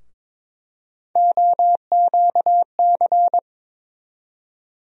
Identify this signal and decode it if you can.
Morse code, heard as a keyed tone, about 22 wpm, 700 Hz. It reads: OQC
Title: USA Call Sign Postfixes - 3 Letters 22wpm